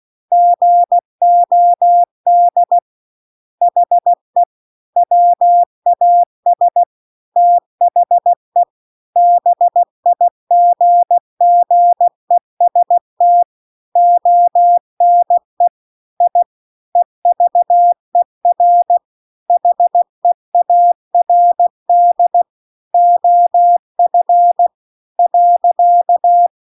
Morse code, heard as a keyed tone, about 16 wpm, 700 Hz. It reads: GOD HE WAS THE BIGGEST ONE I EVER HEARD OF .